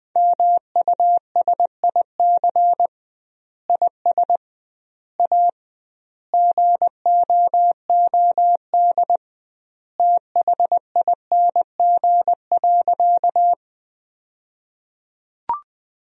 Morse code, heard as a keyed tone, about 21 wpm, 700 Hz. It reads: MUSIC IS A GOOD THING.
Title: 2x Word-Spacing - Sentences from Top 300 Words - 20wpm